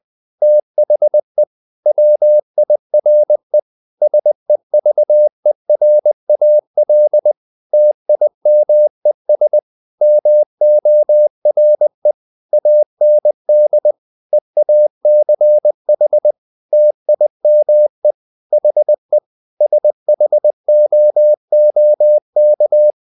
Morse code, heard as a keyed tone, about 20 wpm, 600 Hz. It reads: THE WIRE SEVERAL TIMES MORE AND EACH TIME HE SHOOK
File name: the old man and the sea part 2 20wpm